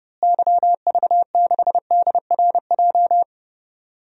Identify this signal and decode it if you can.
Morse code, heard as a keyed tone, about 30 wpm, 700 Hz. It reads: YV6DRJ
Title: DXCC-100 Call Signs - 6 Characters - Repeated 2x - 30wpm